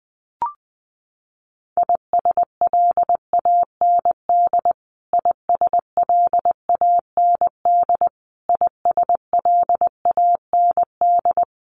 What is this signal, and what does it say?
Morse code, heard as a keyed tone, about 20 wpm, 700 Hz. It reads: ISLAND ISLAND ISLAND
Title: Top 500 Words Repeated 20wpm